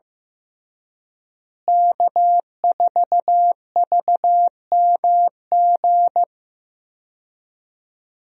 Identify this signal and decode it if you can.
Morse code, heard as a keyed tone, about 15 wpm, 700 Hz. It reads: K4VMG
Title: USA Call Signs - 5 Characters - Repeated 2x - 15wpm